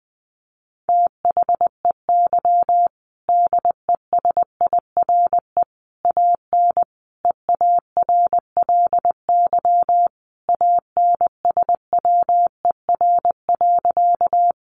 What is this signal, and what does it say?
Morse code, heard as a keyed tone, about 19 wpm, 700 Hz. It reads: THEY DESIRE AN EARLY ANSWER.
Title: Sentences from Top 1100 Words 20wpm